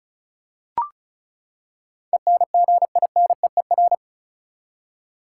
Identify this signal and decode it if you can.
Morse code, heard as a keyed tone, about 34 wpm, 700 Hz. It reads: ENGINEER
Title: K1USN SST QSO Practice 35wpm